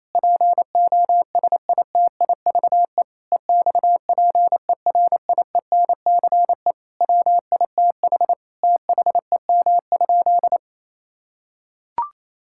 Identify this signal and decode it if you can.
Morse code, heard as a keyed tone, about 28 wpm, 700 Hz. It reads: POSITIVE EXPERIENCE WITH THEM?
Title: Sentences from Top 1000 Words - Review Part 5 - 28wpm